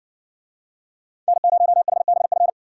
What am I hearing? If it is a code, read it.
Morse code, heard as a keyed tone, about 57 wpm, 700 Hz. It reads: N0HBF